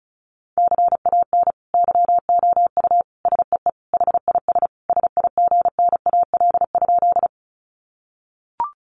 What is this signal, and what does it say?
Morse code, heard as a keyed tone, about 34 wpm, 700 Hz. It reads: CAN YOU SEE HIS SIGNAL?
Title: Sentences from Top 1400 Words 35wpm